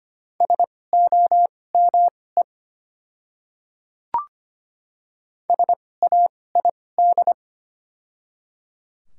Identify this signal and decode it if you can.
Morse code, heard as a keyed tone, about 25 wpm, spaced as for 18 wpm, 700 Hz. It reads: SOME SAID
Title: 2x Character Spacing - Top 100 Words - 25wpm